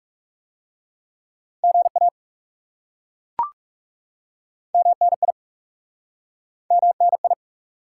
Morse code, heard as a keyed tone, about 40 wpm, 700 Hz. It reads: GA MNI MNI